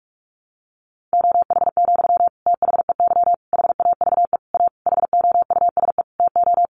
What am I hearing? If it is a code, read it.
Morse code, heard as a keyed tone, about 44 wpm, 700 Hz. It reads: OH, THEY HAVE A HOUSE TO
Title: Sentences from Top 800 Words 45wpm